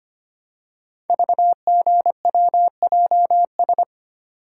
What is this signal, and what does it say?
Morse code, heard as a keyed tone, about 25 wpm, 700 Hz. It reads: VGWJS
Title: Code Groups - 5 Letters - 25wpm